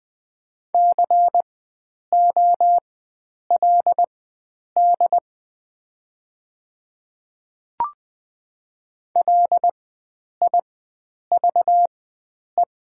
Morse code, heard as a keyed tone, about 20 wpm, spaced as for 9 wpm, 690 Hz. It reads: COLD LIVE
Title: 4x Character Spacing - Top 500 Words - 20wpm